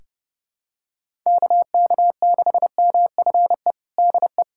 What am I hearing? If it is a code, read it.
Morse code, heard as a keyed tone, about 32 wpm, 700 Hz. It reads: KK6MFE DE
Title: USA Callsign DE Callsign - 6 Characters - 30wpm